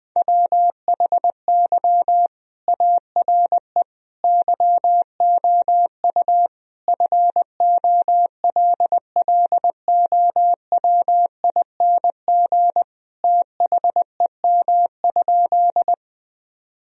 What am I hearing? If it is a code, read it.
Morse code, heard as a keyed tone, about 20 wpm, 700 Hz. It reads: WHY ARE YOU FOLLOWING THEM?
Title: Sentences from Top 1000 Words - Review Part 4 - 20wpm